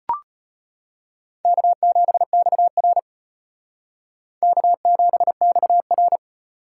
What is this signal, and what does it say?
Morse code, heard as a keyed tone, about 35 wpm, 700 Hz. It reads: K7XR K7XR